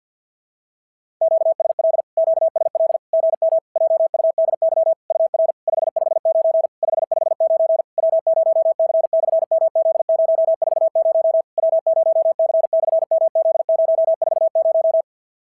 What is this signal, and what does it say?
Morse code, heard as a keyed tone, about 51 wpm, 650 Hz. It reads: QSL QSL GM JUDY UR 559 559 W0C/MZ040 W0C/MZ040
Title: SOTA QSO Practice 50wpm